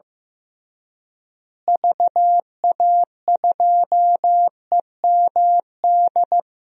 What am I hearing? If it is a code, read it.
Morse code, heard as a keyed tone, about 15 wpm, 700 Hz. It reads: VA2EMD
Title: DXCC-100 Call Signs - 6 Characters - Repeated 3x - 15wpm